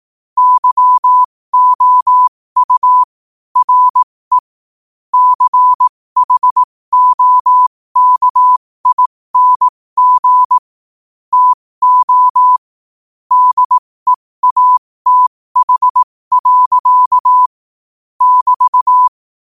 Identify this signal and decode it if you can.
Morse code, heard as a keyed tone, about 18 wpm, spaced as for 15 wpm, 1000 Hz. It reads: YOU RE CHOKING TO DEATH. <BT>
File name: qotdfiles / 15wpm